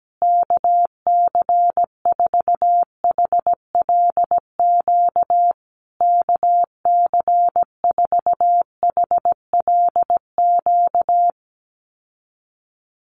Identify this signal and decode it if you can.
Morse code, heard as a keyed tone, about 17 wpm, 700 Hz. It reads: KC4HLQ KC4HLQ
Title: USA Call Signs Repeated 3x 17wpm